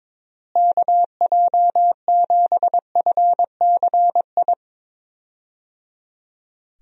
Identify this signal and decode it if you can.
Morse code, heard as a keyed tone, about 22 wpm, 700 Hz. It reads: KJ7FCI
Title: USA Call Signs - 6 Characters - Repeated 3x - 22wpm